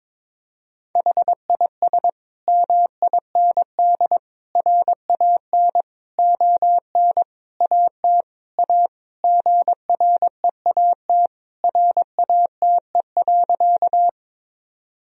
Morse code, heard as a keyed tone, about 22 wpm, 700 Hz. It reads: HIS MIND RAN ON AT A GREAT RATE.
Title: Sentences from Top 1000 Words - Review Part 4 - 22wpm